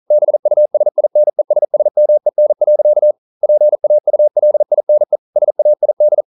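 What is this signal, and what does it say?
Morse code, heard as a keyed tone, about 42 wpm, 600 Hz. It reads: BUSINESSMEN. PAULINE SAID
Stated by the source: Morse code practice at 41 wpm, from CW med Gnister.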